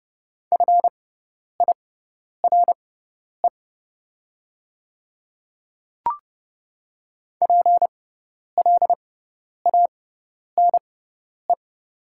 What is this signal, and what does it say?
Morse code, heard as a keyed tone, about 30 wpm, spaced as for 10 wpm, 700 Hz. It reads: FIRE PLANE
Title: 6x Character Spacing - Top 500 Words - 30wpm